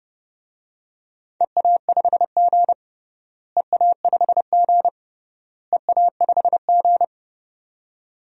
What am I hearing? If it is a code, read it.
Morse code, heard as a keyed tone, about 32 wpm, 700 Hz. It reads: EA5G EA5G EA5G
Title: DXCC-100 Call Signs - 4 Characters - Repeated 3x - 30wpm